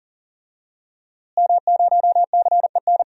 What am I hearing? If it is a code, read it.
Morse code, heard as a keyed tone, about 40 wpm, 700 Hz. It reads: M0CEN
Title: DXCC-100 Call Signs - 5 Characters - Repeated 2x - 40wpm